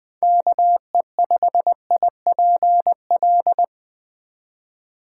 Morse code, heard as a keyed tone, about 20 wpm, 700 Hz. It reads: KE5IPL